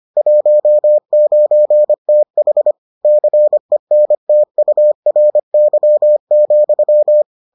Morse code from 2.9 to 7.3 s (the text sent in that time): CENTURY,